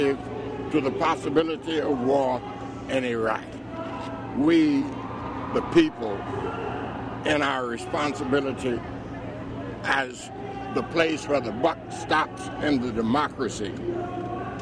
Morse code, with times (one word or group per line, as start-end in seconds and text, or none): none